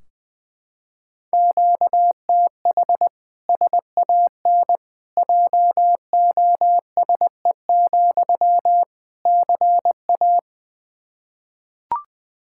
Morse code, heard as a keyed tone, about 20 wpm, 700 Hz. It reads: QTH SAN JOSE, CA
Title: Practice QSO - Exchange 1 - Beginning - 20wpm